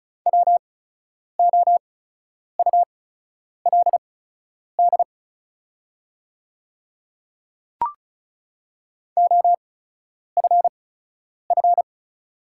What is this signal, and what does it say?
Morse code, heard as a keyed tone, about 36 wpm, spaced as for 10 wpm, 690 Hz. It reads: WOULD OFF
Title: 8x Character Spacing - Top 200 Words - 35wpm